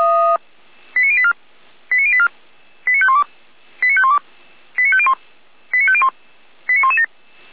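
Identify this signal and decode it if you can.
Five-tone selective call (ZVEI), heard as quick runs of five tones, tones of about 70 ms. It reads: 89084 89084 89512 89512 89501 89501 89117
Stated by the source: ZVEI five-tone selective call 89115 received on SDR.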